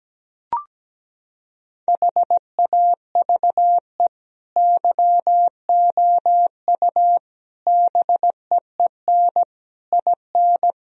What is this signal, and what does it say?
Morse code, heard as a keyed tone, about 17 wpm, 700 Hz. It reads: HAVE YOU BEEN IN
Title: Sentences from Top 1000 Words - Review Part 2 - 17wpm